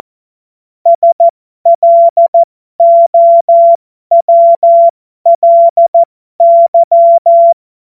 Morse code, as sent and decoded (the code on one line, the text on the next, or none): ... .-.. --- .-- .-.. -.--
SLOWLY